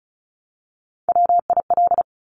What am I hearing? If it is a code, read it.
Morse code, heard as a keyed tone, about 36 wpm, 700 Hz. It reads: WIL
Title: Top Three Letter Combinations in Words 35wpm